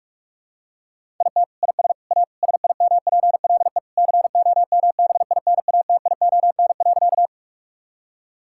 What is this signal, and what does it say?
Morse code, heard as a keyed tone, about 41 wpm, 700 Hz. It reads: IT IS A SIMPLE COMBINATION.
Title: Sentences from Top 1000 Words - Review Part 6 - 45wpm